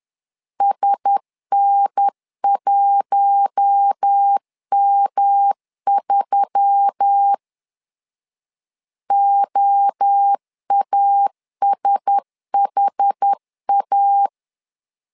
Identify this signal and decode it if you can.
Morse code, heard as a keyed tone, about 10 wpm, 800 Hz. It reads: SN1M3 OASHA
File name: LEZIONI / 4 Q1S3 10wpm 2427